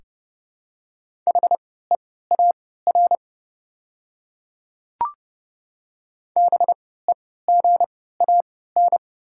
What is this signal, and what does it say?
Morse code, heard as a keyed tone, about 30 wpm, spaced as for 17 wpm, 690 Hz. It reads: HEAR BEGAN
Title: 3x Character Spacing - Top 300 Words - 30wpm